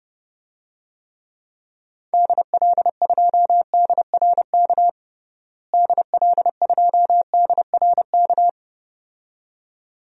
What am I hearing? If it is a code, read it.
Morse code, heard as a keyed tone, about 32 wpm, 700 Hz. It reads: DL2DRK DL2DRK
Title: DXCC-100 Call Signs - 6 Characters - Repeated 2x - 30wpm